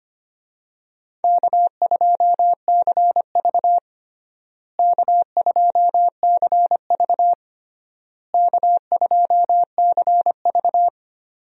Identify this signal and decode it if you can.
Morse code, heard as a keyed tone, about 25 wpm, 700 Hz. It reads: K2CV K2CV K2CV
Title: USA Call Signs - 4 Characters - Repeated 3x - 25wpm